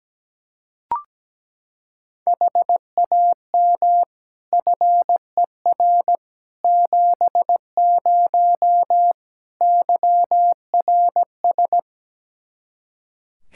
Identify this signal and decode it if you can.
Morse code, heard as a keyed tone, about 17 wpm, 700 Hz. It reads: HAM FER 70 YRS
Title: Practice QSO - Exchange 3 - More Info - 17wpm